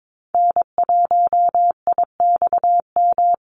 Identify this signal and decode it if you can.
Morse code, heard as a keyed tone, about 22 wpm, 700 Hz. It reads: N1IXM
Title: USA Call Signs 22wpm